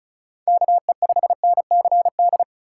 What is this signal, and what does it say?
Morse code, heard as a keyed tone, about 36 wpm, 700 Hz. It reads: KE5NCD
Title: USA Call Signs - 6 Characters - 8x Character Spacing - 35wpm